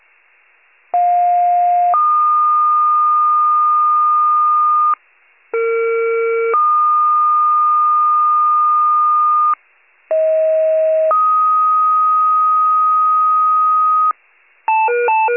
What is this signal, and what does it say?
Two-tone page, heard as a one-second tone, then a three-second tone, about 710 then 1220 Hz, about 460 then 1220 Hz, about 640 then 1250 Hz. It is heard from 0.9 to 14.1 s.
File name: wcalert